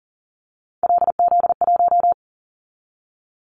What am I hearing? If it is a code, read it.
Morse code, heard as a keyed tone, about 40 wpm, 700 Hz. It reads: LZ1
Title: Callsign Prefixes for Top 50 DXCC - 40wpm